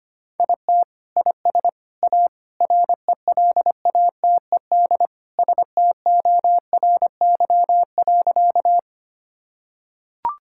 Morse code, heard as a keyed tone, about 25 wpm, 700 Hz. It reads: IT IS A RELATED STORY.
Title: Sentences from Top 1000 Words - Review Part 5 - 25wpm